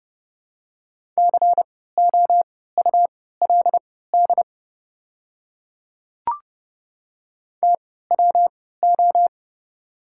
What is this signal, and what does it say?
Morse code, heard as a keyed tone, about 30 wpm, spaced as for 17 wpm, 700 Hz. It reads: COULD TWO
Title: 3x Character Spacing - Top 200 Words - 30wpm